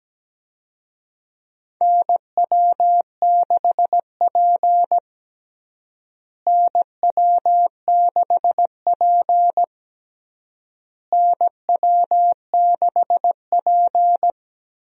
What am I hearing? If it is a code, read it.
Morse code, heard as a keyed tone, about 17 wpm, 700 Hz. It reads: NW6P NW6P NW6P
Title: USA Call Signs - 4 Characters - Repeated 3x - 17wpm